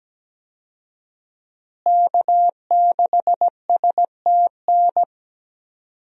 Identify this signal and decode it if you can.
Morse code, heard as a keyed tone, about 17 wpm, 700 Hz. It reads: K6STN
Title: USA Call Signs - 5 Characters - Repeated 2x - 17wpm